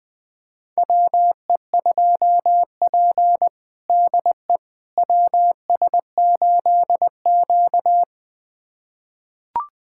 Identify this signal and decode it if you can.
Morse code, heard as a keyed tone, about 21 wpm, 700 Hz. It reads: WE2P DE WS8Q
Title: USA Callsign DE Callsign - 4 Characters - 20wpm